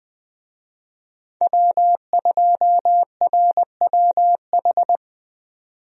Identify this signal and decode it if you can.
Morse code, heard as a keyed tone, about 20 wpm, 700 Hz. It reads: W2RWH
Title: USA Call Signs - 5 Characters - 20wpm